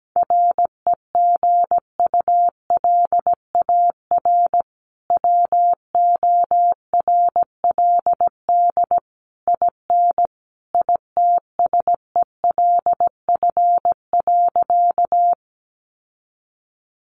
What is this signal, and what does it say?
Morse code, heard as a keyed tone, about 17 wpm, 700 Hz. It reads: REGULAR WORLD IN ITSELF.
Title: Sentences from Top 900 Words 17wpm